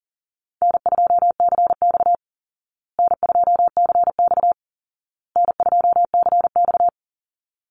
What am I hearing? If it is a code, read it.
Morse code, heard as a keyed tone, about 40 wpm, 700 Hz. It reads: N2CX N2CX N2CX
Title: Call Signs Repeated 2x - 40wpm